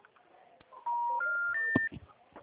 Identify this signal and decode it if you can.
Special information tone.